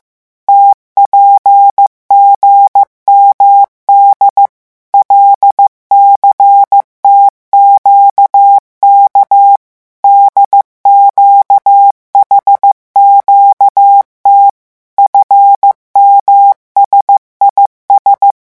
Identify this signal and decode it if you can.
Morse code, heard as a keyed tone, about 15 wpm, 800 Hz. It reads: TPGMD LCTQK DQHQT FMSIS
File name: morse mp3 / revision11 lettres vitesse 14